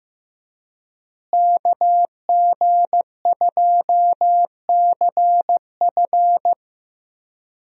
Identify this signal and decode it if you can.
Morse code, heard as a keyed tone, about 15 wpm, 700 Hz. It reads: KG2CF
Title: USA Call Signs - 5 Characters - Repeated 3x - 15wpm